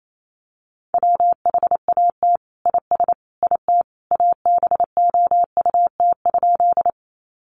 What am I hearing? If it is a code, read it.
Morse code, heard as a keyed tone, about 28 wpm, 700 Hz. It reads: WHAT IS IT ABOUT?
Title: Sentences from Top 100 Words 28wpm